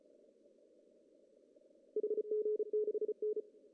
Morse code, heard as a keyed tone, about 32 wpm, 400 Hz. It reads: HG6N
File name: CW / cqww-cw-k5tr